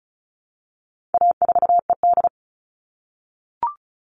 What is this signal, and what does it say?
Morse code, heard as a keyed tone, about 34 wpm, 700 Hz. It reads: A4ED
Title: All Cut Numbers - 4-Digit 35wpm